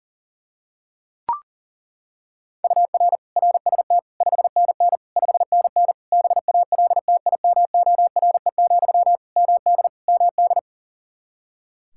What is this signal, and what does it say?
Morse code, heard as a keyed tone, about 40 wpm, 700 Hz. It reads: UR RST 5NN 5NN BALTIMORE, MD MD
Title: Practice QSO - Exchange 1 - Beginning - 40wpm